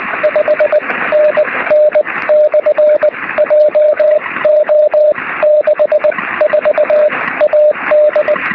Morse code, heard as a keyed tone, about 20 wpm, spaced as for 16 wpm, 600 Hz, with radio noise. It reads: 5NN/JO64AD
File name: sm7lcb / 04-dl0vv 23cm